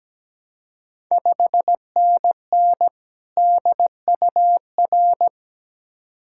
Morse code, heard as a keyed tone, about 17 wpm, 700 Hz. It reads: 5NN DUR
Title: Ontario QSO Party Exchange 17wpm